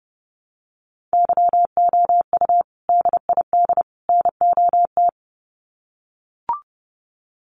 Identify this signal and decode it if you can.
Morse code, heard as a keyed tone, about 30 wpm, 700 Hz. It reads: YOU DID NOT
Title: Sets of 3 Words 30wpm